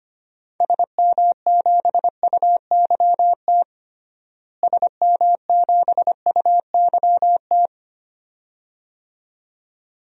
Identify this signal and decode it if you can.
Morse code, heard as a keyed tone, about 25 wpm, 700 Hz. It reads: SM7UYT SM7UYT
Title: DXCC-100 Call Signs - 6 Characters - Repeated 2x - 25wpm